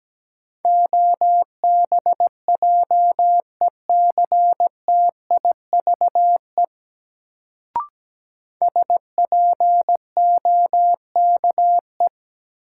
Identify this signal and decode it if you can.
Morse code, heard as a keyed tone, about 17 wpm, 700 Hz. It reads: OBJECTIVE SPOKE
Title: Sentences from Top 1200 Words - New Word Drill - 17wpm